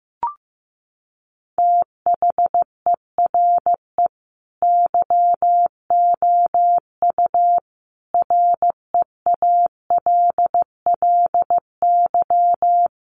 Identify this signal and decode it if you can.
Morse code, heard as a keyed tone, about 15 wpm, 700 Hz. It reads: THERE YOU REALLY
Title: Sentences from Top 900 Words 15wpm